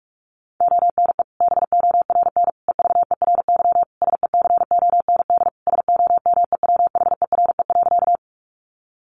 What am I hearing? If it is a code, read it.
Morse code, heard as a keyed tone, about 44 wpm, 700 Hz. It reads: ONE BORN EVERY SECOND SOMEWHERE.
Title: Sentences from Top 1400 Words 45wpm